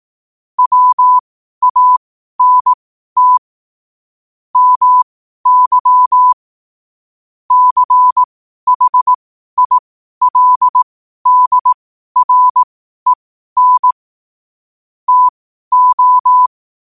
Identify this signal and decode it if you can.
Morse code, heard as a keyed tone, about 18 wpm, spaced as for 12 wpm, 1000 Hz. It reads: WANT MY CHILDREN TO